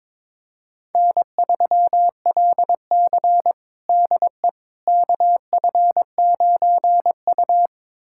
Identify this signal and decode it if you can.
Morse code, heard as a keyed tone, about 22 wpm, 700 Hz. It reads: N3LC DE KF9U
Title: USA Callsign DE Callsign - 4 Characters - 22wpm